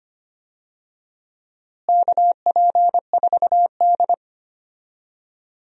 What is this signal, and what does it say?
Morse code, heard as a keyed tone, about 25 wpm, 700 Hz. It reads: KP4D
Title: DXCC-100 Call Signs - 4 Characters - Repeated 2x - 25wpm